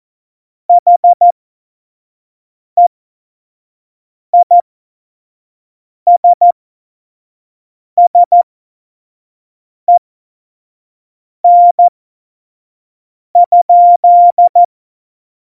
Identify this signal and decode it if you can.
Morse code, heard as a keyed tone, about 14 wpm, spaced as for 5 wpm, 700 Hz. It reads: HEISSEN?
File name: Morsekurs / Witze / Witz 06. Oktober 2020, 25 F70